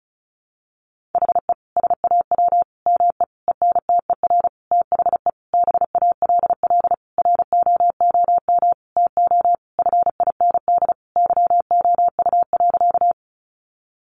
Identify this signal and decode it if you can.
Morse code, heard as a keyed tone, about 34 wpm, 700 Hz. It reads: HE SAW ME ENTER THE BALL ROOM TO FIND YOU.
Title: Sentences from Top 1300 Words 35wpm